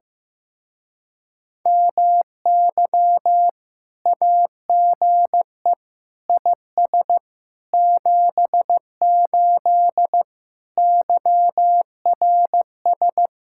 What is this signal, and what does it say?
Morse code, heard as a keyed tone, about 15 wpm, 700 Hz. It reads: MY AGE IS 78 YRS
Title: Speed-Racing Common QSO Phrases Encore - 15wpm